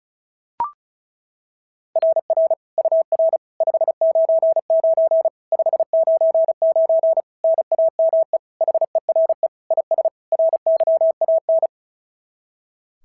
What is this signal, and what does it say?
Morse code, heard as a keyed tone, about 36 wpm, 690 Hz. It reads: RR UR 599 599 NAME HERE IS RYAN
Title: SOTA QSO Practice 35wpm